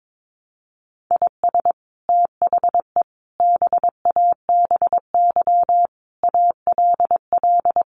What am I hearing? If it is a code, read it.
Morse code, heard as a keyed tone, about 22 wpm, 700 Hz. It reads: IS THE BABY ALL
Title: Sentences from Top 1300 Words 22wpm